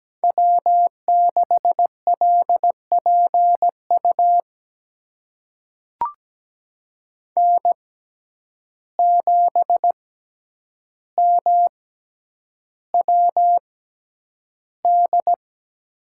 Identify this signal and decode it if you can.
Morse code, heard as a keyed tone, about 17 wpm, spaced as for 9 wpm, 700 Hz. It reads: W6LPU N7MWD
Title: USA Call Signs - 5 Characters - 6x Character Spacing - 17wpm